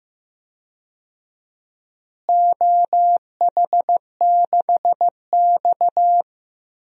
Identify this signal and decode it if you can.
Morse code, heard as a keyed tone, about 15 wpm, 700 Hz. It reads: OH6X
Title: DXCC-100 Call Signs - 4 Characters - Repeated 3x - 15wpm